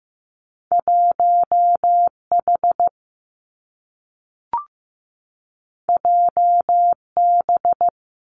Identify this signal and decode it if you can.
Morse code, heard as a keyed tone, about 15 wpm, 700 Hz. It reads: 1H JB
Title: Two Challenging Letter-Number Combinations 15wpm